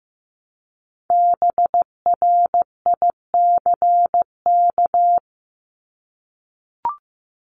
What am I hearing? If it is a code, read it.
Morse code, heard as a keyed tone, about 15 wpm, 700 Hz. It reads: BRICK
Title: Top 1751-2000 Words 15wpm